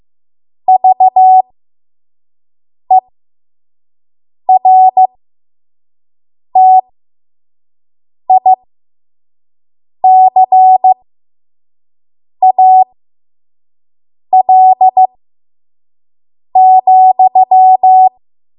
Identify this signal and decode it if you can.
Morse code, heard as a keyed tone, about 15 wpm, spaced as for 5 wpm, 750 Hz. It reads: VERTICAL,
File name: Archive / 5 WPM / 05WPM